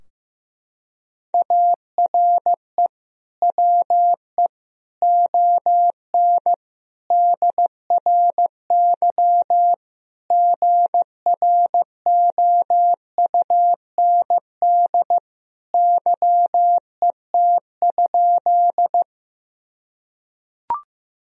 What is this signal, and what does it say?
Morse code, heard as a keyed tone, about 15 wpm, 700 Hz. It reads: ARE WE ON DRY GROUND YET?
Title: Sentences from Top 1000 Words - Review Part 3 - 15wpm